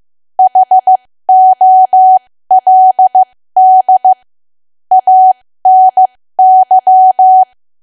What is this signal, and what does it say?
Morse code, heard as a keyed tone, about 15 wpm, spaced as for 13 wpm, 750 Hz. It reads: HOLD ANY
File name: cwa / W1AW-Level2 / 13WPM